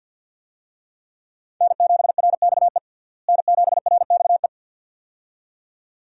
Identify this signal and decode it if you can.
Morse code, heard as a keyed tone, about 51 wpm, 700 Hz. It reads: N7RXE N7RXE
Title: USA Call Signs - 5 Characters - Repeated 2x - 50wpm